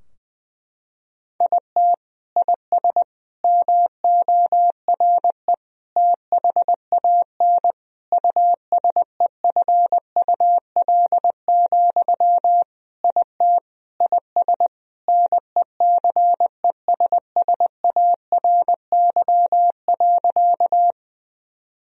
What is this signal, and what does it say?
Morse code, heard as a keyed tone, about 20 wpm, 700 Hz. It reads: IT IS MORE THAN USEFUL, IT IS NECESSARY.
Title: Sentences from Top 1000 Words - Review Part 4 - 20wpm